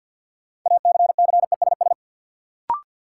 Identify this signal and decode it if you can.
Morse code, heard as a keyed tone, about 51 wpm, 700 Hz. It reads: ACCESS
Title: Sentences from Top 1000 Words - Review Part 4 - New Word Drill - 50wpm